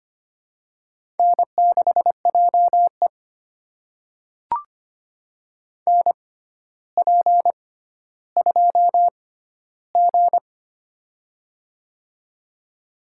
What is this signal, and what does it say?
Morse code, heard as a keyed tone, about 25 wpm, spaced as for 13 wpm, 700 Hz. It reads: N6JE NP2G
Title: USA Call Signs - 4 Characters - 6x Character Spacing - 25wpm